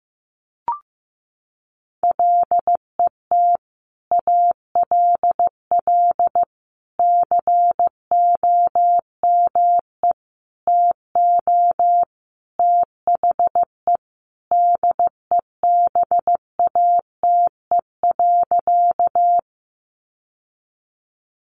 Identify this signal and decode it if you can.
Morse code, heard as a keyed tone, about 15 wpm, 700 Hz. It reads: LET ALL COME TO THE DEBATE.
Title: Sentences from Top 1100 Words 15wpm